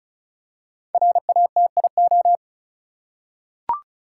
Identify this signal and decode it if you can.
Morse code, heard as a keyed tone, about 36 wpm, 700 Hz. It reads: RATIO
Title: Five Letter Words - 35wpm